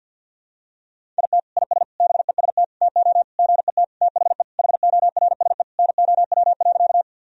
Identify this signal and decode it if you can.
Morse code, heard as a keyed tone, about 51 wpm, 700 Hz. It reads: IT IS BEST TO GET THE HORSE NOW.
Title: Sentences from Top 1000 Words - Review Part 1 - 50wpm